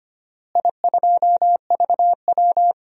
Morse code, heard as a keyed tone, about 25 wpm, 700 Hz. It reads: I2VW
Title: DXCC-100 Call Signs - 4 Characters - Repeated 3x - 25wpm